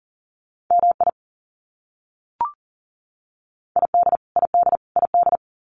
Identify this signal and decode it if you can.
Morse code, heard as a keyed tone, about 40 wpm, 700 Hz. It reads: MI ID ID ID
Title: US States Repeated 3x 40wpm